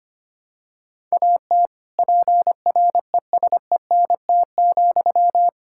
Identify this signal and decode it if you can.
Morse code, heard as a keyed tone, about 25 wpm, 700 Hz. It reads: AT PRESENT,
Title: Sentences from Top 1000 Words - Review Part 5 - 25wpm